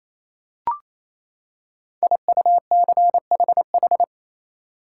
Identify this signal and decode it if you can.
Morse code, heard as a keyed tone, about 28 wpm, 700 Hz. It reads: IUCHH
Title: Code Groups - 5 Letters - 28wpm